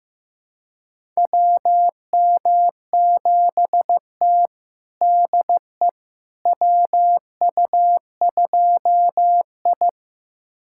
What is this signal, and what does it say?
Morse code, heard as a keyed tone, about 15 wpm, 700 Hz. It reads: WM7T DE WU2I
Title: USA Callsign DE Callsign - 4 Characters - 15wpm